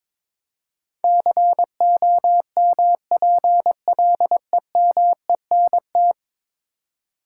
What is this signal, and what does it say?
Morse code, heard as a keyed tone, about 22 wpm, 700 Hz. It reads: COMPLEMENT
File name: practice / 22wpm / 10 Letter Words - First 3 Letters Given - 22wpm